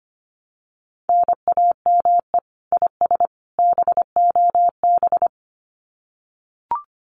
Morse code, heard as a keyed tone, about 25 wpm, 700 Hz. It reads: NAME IS BOB
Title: Common QSO Phrases 25wpm